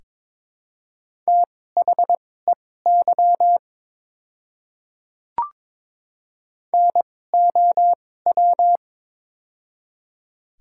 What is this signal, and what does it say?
Morse code, heard as a keyed tone, about 22 wpm, spaced as for 16 wpm, 700 Hz. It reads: THEY NOW
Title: 2x Character Spacing - Top 100 Words - 22wpm